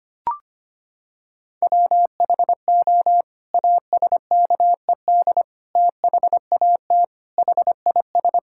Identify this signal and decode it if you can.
Morse code, heard as a keyed tone, about 25 wpm, 700 Hz. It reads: WHO ASKED THAT HIS
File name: Sets of 9 Words - 25wpm